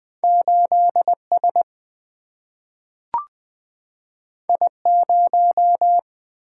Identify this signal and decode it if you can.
Morse code, heard as a keyed tone, about 21 wpm, 700 Hz. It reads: 8 S I 0
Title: Single-Letter Single-Number 20wpm